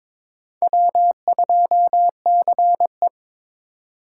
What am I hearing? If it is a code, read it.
Morse code, heard as a keyed tone, about 22 wpm, 700 Hz. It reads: W2CE
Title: USA Call Signs - 2x Character Spacing - 22wpm